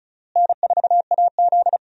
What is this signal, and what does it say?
Morse code, heard as a keyed tone, about 36 wpm, 700 Hz. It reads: N4AZ
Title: USA Call Signs - 4 Characters - Repeated 2x - 35wpm